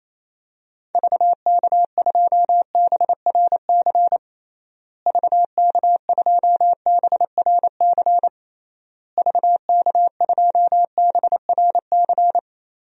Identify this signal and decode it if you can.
Morse code, heard as a keyed tone, about 28 wpm, 700 Hz. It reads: VK2BRC VK2BRC VK2BRC
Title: DXCC-100 Call Signs - 6 Characters - Repeated 3x - 28wpm